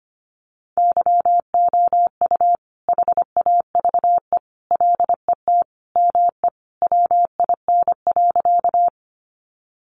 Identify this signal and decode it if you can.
Morse code, heard as a keyed tone, about 25 wpm, 700 Hz. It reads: YOU HAVE LET ME WIN.
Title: Sentences from Top 1500 Words 25wpm